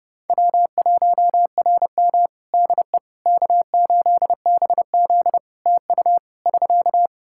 Morse code, heard as a keyed tone, about 32 wpm, 700 Hz. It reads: W1RM DE K8BZ TU <SK>
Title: Practice QSO - Exchange 4 - Ending - 30wpm